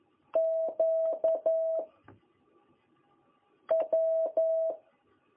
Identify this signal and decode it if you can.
Morse code, heard as a keyed tone, about 11 wpm, 660 Hz. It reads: Q W
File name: jh8cbh / 2021cw